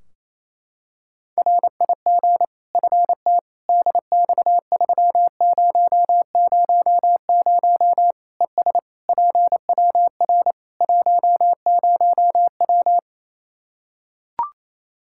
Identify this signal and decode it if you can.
Morse code, heard as a keyed tone, about 28 wpm, 700 Hz. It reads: RIG FT DX3000 ES PWR 10W
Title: Practice QSO - Exchange 2 - Info - 28wpm